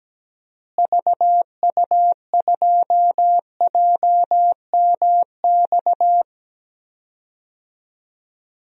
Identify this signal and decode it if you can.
Morse code, heard as a keyed tone, about 17 wpm, 700 Hz. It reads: VU2JMX
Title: DXCC-100 Call Signs - 6 Characters - Repeated 3x - 17wpm